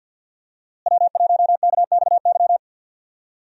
Morse code, heard as a keyed tone, about 52 wpm, 700 Hz. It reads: W1KXY
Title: USA Call Signs - 5 Characters - Repeated 3x - 50wpm